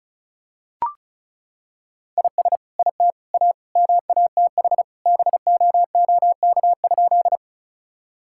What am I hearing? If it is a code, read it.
Morse code, heard as a keyed tone, about 36 wpm, 700 Hz. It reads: IS IT A MATH BOOK?
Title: Sentences from Top 1000 Words - Review Part 6 - 35wpm